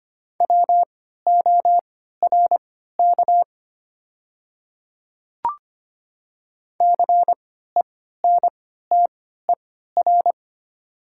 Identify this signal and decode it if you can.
Morse code, heard as a keyed tone, about 25 wpm, spaced as for 14 wpm, 690 Hz. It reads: WORK CENTER
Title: 3x Character Spacing - Top 500 Words - 25wpm